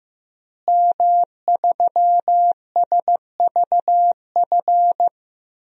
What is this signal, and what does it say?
Morse code, heard as a keyed tone, about 15 wpm, 700 Hz. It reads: M3SVF
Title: DXCC-100 Call Signs - 5 Characters - Repeated 2x - 15wpm